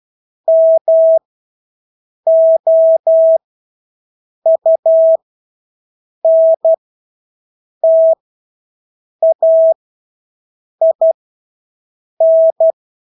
Morse code, heard as a keyed tone, about 12 wpm, spaced as for 6 wpm, 650 Hz. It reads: MOUNTAIN
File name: Engelske ord 12-6 wpm